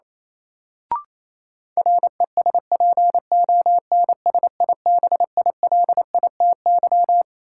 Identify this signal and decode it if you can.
Morse code, heard as a keyed tone, about 28 wpm, 700 Hz. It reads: RESPONSIBILITY
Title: Sentences from Top 1200 Words - New Word Drill Alt - 28wpm